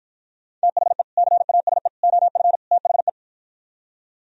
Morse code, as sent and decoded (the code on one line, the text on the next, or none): - .... / -.-. .- ... / --- ..-. / - .... .
THE CASE OF THE